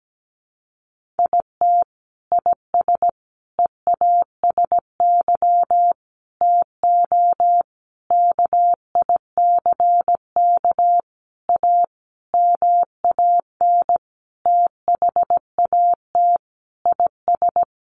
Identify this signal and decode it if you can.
Morse code, heard as a keyed tone, about 17 wpm, 700 Hz. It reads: IT IS EASY TO KICK A MAN THAT IS
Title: Sentences from Top 1300 Words 17wpm